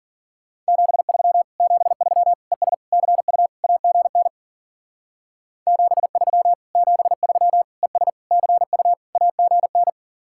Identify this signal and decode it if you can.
Morse code, heard as a keyed tone, about 50 wpm, 700 Hz. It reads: 73 73 ES CU AGN 73 73 ES CU AGN